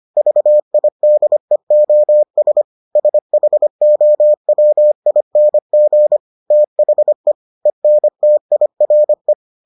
Morse code, heard as a keyed tone, about 25 wpm, 600 Hz. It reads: VIDEOS SHOWING THE ENTIRE